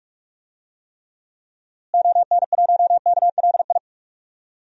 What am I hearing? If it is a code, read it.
Morse code, heard as a keyed tone, about 46 wpm, 700 Hz. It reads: ON1KLI
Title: DXCC-100 Call Signs - 6 Characters - Repeated 3x - 45wpm